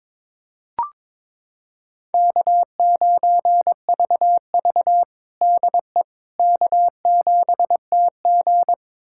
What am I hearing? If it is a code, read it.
Morse code, heard as a keyed tone, about 22 wpm, 700 Hz. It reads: K9VV DE K7TG
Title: USA Callsign DE Callsign - 4 Characters - 22wpm